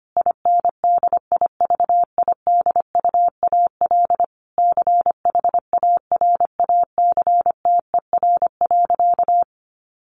Morse code, heard as a keyed tone, about 25 wpm, 700 Hz. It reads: INDIVIDUAL CHARACTER.
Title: Sentences from Top 1500 Words 25wpm